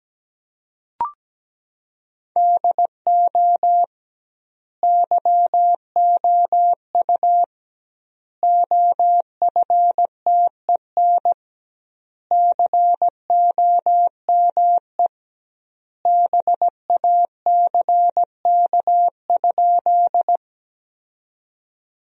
Morse code, heard as a keyed tone, about 17 wpm, 700 Hz. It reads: DO YOU OFTEN COME BACK?
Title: 2x Word-Spacing - Sentences from Top 300 Words - 17wpm